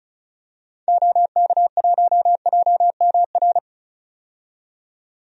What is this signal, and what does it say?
Morse code, heard as a keyed tone, about 34 wpm, 700 Hz. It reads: OK1JMR
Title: DXCC-100 Call Signs - 6 Characters - 35wpm